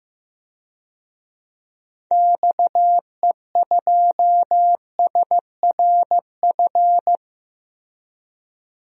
Morse code, heard as a keyed tone, about 15 wpm, 700 Hz. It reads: XE2SRF
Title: DXCC-100 Call Signs - 6 Characters - Repeated 3x - 15wpm